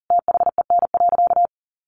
Morse code, heard as a keyed tone, about 40 wpm, 700 Hz. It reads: THEN.
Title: Sentences from Top 200 Words 40wpm